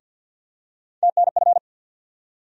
Morse code, heard as a keyed tone, about 47 wpm, 700 Hz. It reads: TNF
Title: USA Call Sign Postfixes - 3 Letters 50wpm